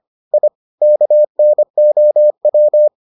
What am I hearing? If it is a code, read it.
Morse code, heard as a keyed tone, about 25 wpm, 600 Hz. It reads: I KNOW